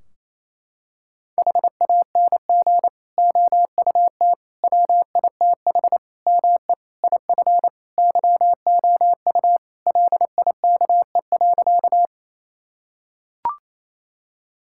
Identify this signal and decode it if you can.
Morse code, heard as a keyed tone, about 27 wpm, 700 Hz. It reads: HANG OUT WITH ME IF YOU LIKE.
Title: Sentences from Top 1600 Words 28wpm